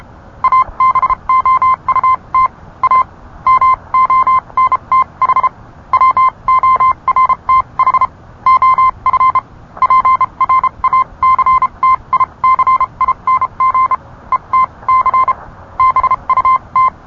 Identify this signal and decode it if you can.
Morse code, heard as a keyed tone, about 30 wpm, spaced as for 25 wpm, 1030 Hz. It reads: ABOUT A MONTH WORTH OF PRACTICING ETC BUT